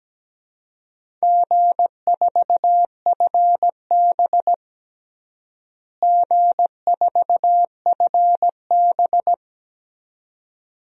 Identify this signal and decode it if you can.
Morse code, heard as a keyed tone, about 17 wpm, 700 Hz. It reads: G4FB G4FB
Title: DXCC-100 Call Signs - 4 Characters - Repeated 2x - 17wpm